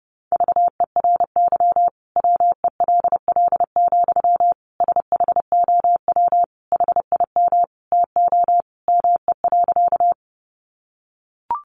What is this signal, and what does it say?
Morse code, heard as a keyed tone, about 30 wpm, 700 Hz. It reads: VERY WELL, SHOW HIM TO ME.
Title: Sentences from Top 200 Words 30wpm